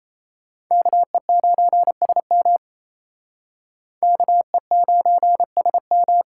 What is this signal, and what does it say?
Morse code, heard as a keyed tone, about 37 wpm, 700 Hz. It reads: KE9SM KE9SM